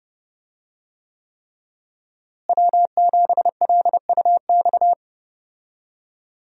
Morse code, heard as a keyed tone, about 32 wpm, 700 Hz. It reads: W7LUX
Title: USA Call Signs - 5 Characters - Repeated 3x - 30wpm